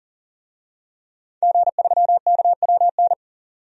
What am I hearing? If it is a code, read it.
Morse code, heard as a keyed tone, about 40 wpm, 700 Hz. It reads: G3KWN
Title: DXCC-100 Call Signs - 5 Characters - 40wpm